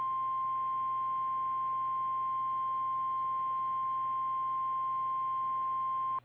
Emergency Alert System (SAME) tones (the 1050 Hz attention tone). It is heard from 0.8 to 5.1 s.
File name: eas / KKPRKXPN / KKPRKXPN